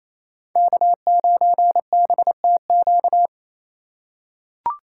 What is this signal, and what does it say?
Morse code, heard as a keyed tone, about 28 wpm, 700 Hz. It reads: K9BTQ
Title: K1USN SST QSO Practice - 28wpm